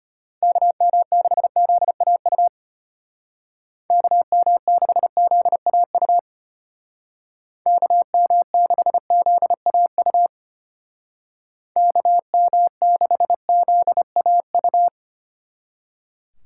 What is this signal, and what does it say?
Morse code, heard as a keyed tone, about 31 wpm, 700 Hz. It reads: KM6ZAU KM6ZAU KM6ZAU KM6ZAU
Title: Speed Racing - USA 2x3 Call Signs 25wpm